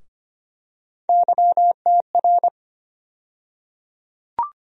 Morse code, heard as a keyed tone, about 25 wpm, 700 Hz. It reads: YTR
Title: USA Call Sign Postfixes - 3 Letters 25wpm